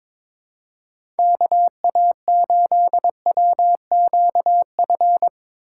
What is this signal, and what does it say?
Morse code, heard as a keyed tone, about 22 wpm, 700 Hz. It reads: KA8WQF
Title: USA Call Signs - 6 Characters - 3x Character Spacing - 22wpm